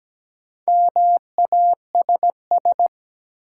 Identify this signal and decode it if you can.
Morse code, heard as a keyed tone, about 17 wpm, 700 Hz. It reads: MASS
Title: Sets of 5 Words Encore - New Word Drill - 17wpm